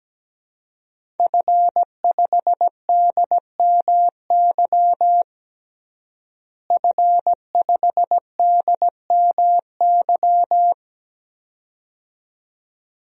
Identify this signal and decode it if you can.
Morse code, heard as a keyed tone, about 17 wpm, 700 Hz. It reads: F5DMY F5DMY
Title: DXCC-100 Call Signs - 5 Characters - Repeated 2x - 17wpm